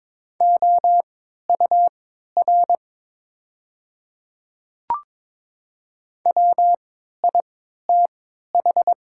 Morse code, heard as a keyed tone, about 22 wpm, spaced as for 12 wpm, 690 Hz. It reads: OUR WITH
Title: 3x Character Spacing - Top 300 Words - 22wpm